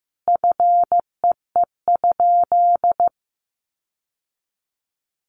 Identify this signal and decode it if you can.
Morse code, heard as a keyed tone, about 15 wpm, 700 Hz. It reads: FEE?
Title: Sentences from Top 1500 Words 15wpm